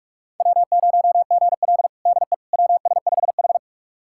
Morse code, heard as a keyed tone, about 42 wpm, 700 Hz. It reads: W0GL DE WS5H
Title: USA Callsign DE Callsign - 4 Characters - 45wpm